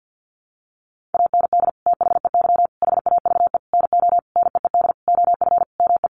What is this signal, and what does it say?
Morse code, heard as a keyed tone, about 50 wpm, 700 Hz. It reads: AND THEY HAVE NO NEED OF ME
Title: Sentences from Top 1400 Words 50wpm